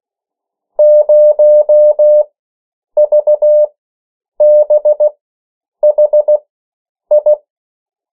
Morse code, heard as a keyed tone, about 16 wpm, spaced as for 8 wpm, 600 Hz. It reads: 0VBHI
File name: morse mp3 / 14 urok 13